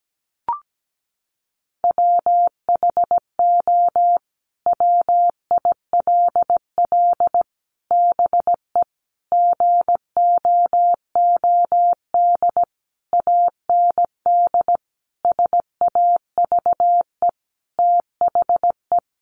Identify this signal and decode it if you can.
Morse code, heard as a keyed tone, about 17 wpm, 700 Hz. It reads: WHO WILL BE GOOD AND SAVE THE
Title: Sentences from Top 1100 Words 17wpm